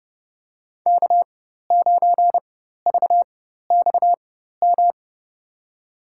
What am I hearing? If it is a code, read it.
Morse code, heard as a keyed tone, about 30 wpm, spaced as for 14 wpm, 700 Hz. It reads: K9VXM
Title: USA Call Signs - 5 Characters - 4x Character Spacing - 30wpm